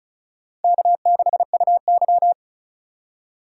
Morse code, heard as a keyed tone, about 34 wpm, 700 Hz. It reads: K6UY
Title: USA Call Signs - 4 Characters - 3x Character Spacing - 35wpm